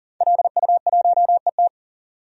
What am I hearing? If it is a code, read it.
Morse code, heard as a keyed tone, about 39 wpm, 700 Hz. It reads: LU1ET
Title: DXCC-100 Call Signs - 5 Characters - 40wpm